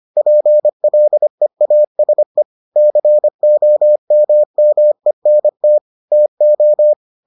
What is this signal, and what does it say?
Morse code, heard as a keyed tone, about 25 wpm, 600 Hz. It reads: PLEASE COMMENT TO